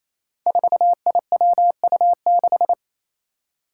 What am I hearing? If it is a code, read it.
Morse code, heard as a keyed tone, about 29 wpm, 700 Hz. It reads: 4IWU6
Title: Five Challenging Letter-Number Combinations 28wpm